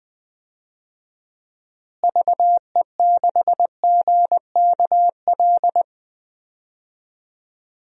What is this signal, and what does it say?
Morse code, heard as a keyed tone, about 21 wpm, 700 Hz. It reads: VE6GKL
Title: DXCC-100 Call Signs - 6 Characters - Repeated 2x - 20wpm